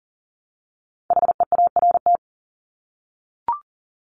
Morse code, heard as a keyed tone, about 38 wpm, 700 Hz. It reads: HEART
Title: Top 1001-1250 Words 40wpm